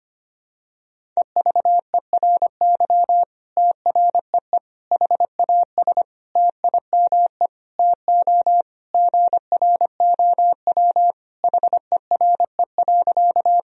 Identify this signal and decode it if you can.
Morse code, heard as a keyed tone, about 25 wpm, 700 Hz. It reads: EVERY TREE HAS TIME TO GROW HERE.
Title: Sentences from Top 1000 Words - Review Part 1 - 25wpm